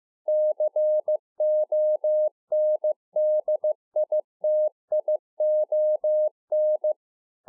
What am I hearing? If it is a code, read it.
Morse code, heard as a keyed tone, about 15 wpm, 610 Hz. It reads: CONDITION